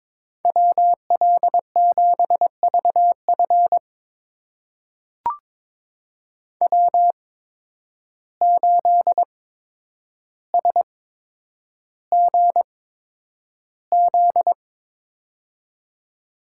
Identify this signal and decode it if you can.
Morse code, heard as a keyed tone, about 22 wpm, spaced as for 9 wpm, 700 Hz. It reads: WL7VF W8SGZ
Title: USA Call Signs - 5 Characters - 8x Character Spacing - 22wpm